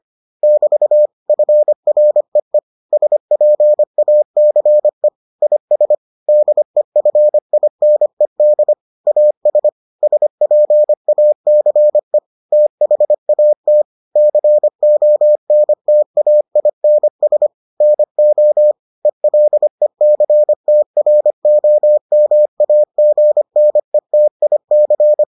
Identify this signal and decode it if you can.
Morse code, heard as a keyed tone, about 25 wpm, 600 Hz. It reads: <BT> FREE SPACE IS DEFINED AS SPACE THAT CONTAINS NO ELECTROMAGNETIC